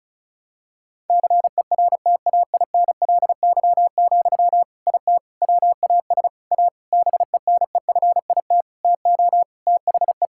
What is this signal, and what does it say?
Morse code, heard as a keyed tone, about 34 wpm, 700 Hz. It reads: CERTAINLY, IT WAS A BENEFIT TO THE
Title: Sentences from Top 1000 Words - Review Part 6 - 35wpm